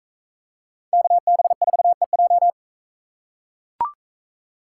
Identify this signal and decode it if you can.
Morse code, heard as a keyed tone, about 41 wpm, 700 Hz. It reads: KB4EJ